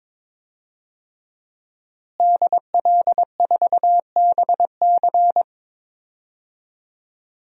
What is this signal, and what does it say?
Morse code, heard as a keyed tone, about 22 wpm, 700 Hz. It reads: DL4BC
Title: DXCC-100 Call Signs - 5 Characters - Repeated 2x - 22wpm